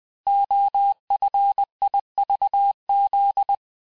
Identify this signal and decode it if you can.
Morse code, heard as a keyed tone, about 20 wpm, 800 Hz. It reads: OFIVZ
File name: cours-858-texte-chiffre-vitesse-19mpm